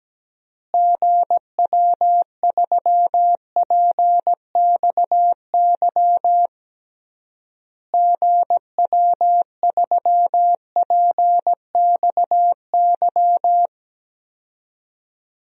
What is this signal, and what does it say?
Morse code, heard as a keyed tone, about 17 wpm, 700 Hz. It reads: GW3PXY GW3PXY
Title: DXCC-100 Call Signs - 6 Characters - Repeated 3x - 17wpm